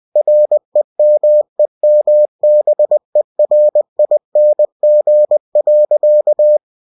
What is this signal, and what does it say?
Morse code, heard as a keyed tone, about 20 wpm, 600 Hz. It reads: REMEMBERING.
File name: cw-books / winnie-the-pooh / winnie-20wpm-page-0011